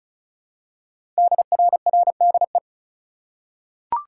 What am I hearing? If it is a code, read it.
Morse code, heard as a keyed tone, about 36 wpm, 700 Hz. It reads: DRRDE
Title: Code Groups - 5 Letters - 35wpm